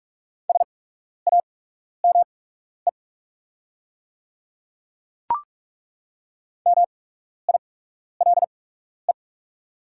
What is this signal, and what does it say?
Morse code, heard as a keyed tone, about 43 wpm, spaced as for 12 wpm, 690 Hz. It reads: SAME MILE
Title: 8x Character Spacing - Top 300 Words - 45wpm